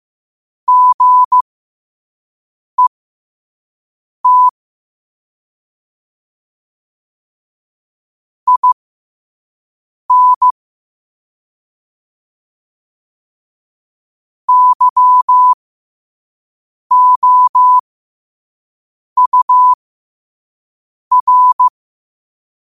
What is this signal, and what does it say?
Morse code, heard as a keyed tone, about 15 wpm, spaced as for 5 wpm, 1000 Hz. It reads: GET IN YOUR
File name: qotdfiles / 5wpm